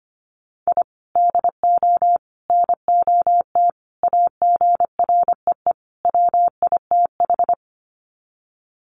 Morse code, heard as a keyed tone, about 25 wpm, 700 Hz. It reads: I DO NOT AGREE WITH
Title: Sets of 5 Words Encore 25wpm